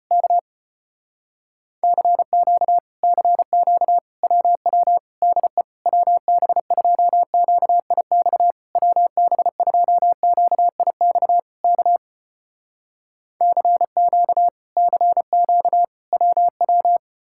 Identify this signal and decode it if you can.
Morse code, heard as a keyed tone, about 35 wpm, 700 Hz. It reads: K CQ CQ WW DE WB2QIX WB2QIX K CQ CQ WW